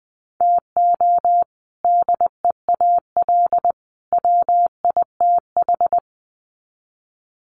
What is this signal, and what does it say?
Morse code, heard as a keyed tone, about 20 wpm, 700 Hz. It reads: TO DEAL WITH
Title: Sets of 3 Words 20wpm